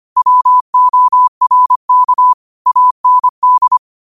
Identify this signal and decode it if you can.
Morse code, heard as a keyed tone, about 25 wpm, 1000 Hz. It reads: WORK AND
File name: qotdfiles / 25wpm